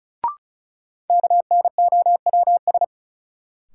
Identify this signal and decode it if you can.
Morse code, heard as a keyed tone, about 36 wpm, 700 Hz. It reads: KNOWS
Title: Sentences from Top 1000 Words - Review Part 1 - New Word Drill - 35wpm